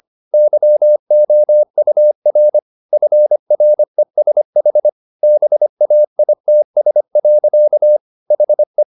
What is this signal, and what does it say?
Morse code, heard as a keyed tone, about 25 wpm, 600 Hz. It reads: YOUR FRESH BAITS. HE
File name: the old man and the sea part 1 25wpm